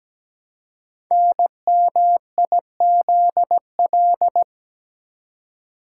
Morse code, heard as a keyed tone, about 17 wpm, 700 Hz. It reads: NMIZL
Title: Code Groups - 5 Letters - 17wpm